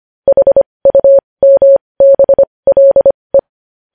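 Morse code, heard as a keyed tone, about 25 wpm, spaced as for 20 wpm, 550 Hz. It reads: HUMBLE
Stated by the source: Morse code news headlines sent at 20 words per minute.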